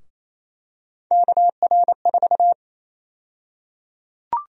Morse code, heard as a keyed tone, about 28 wpm, 700 Hz. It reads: KR4
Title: USA Call Sign Prefixes - 28wpm